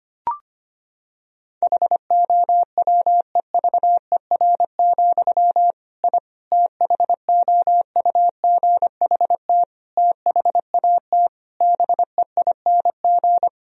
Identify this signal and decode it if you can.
Morse code, heard as a keyed tone, about 25 wpm, 700 Hz. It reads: HOWEVER, I THOUGHT THAT BEING
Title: Sentences from Top 1000 Words - Review Part 3 - 25wpm